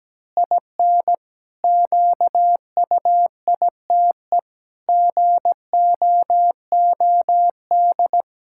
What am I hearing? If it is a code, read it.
Morse code, heard as a keyed tone, about 17 wpm, 700 Hz. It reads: IN QUITE GOOD
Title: Pangram Challenge v2 - 17wpm